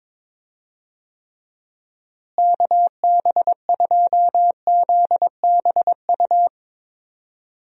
Morse code, heard as a keyed tone, about 22 wpm, 700 Hz. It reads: KB2ZBU
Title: USA Call Signs - 6 Characters - Repeated 3x - 22wpm